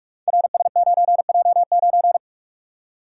Morse code, heard as a keyed tone, about 44 wpm, 690 Hz. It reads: RS9J9
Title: Code Groups - 5 Letter:Numbers - 45wpm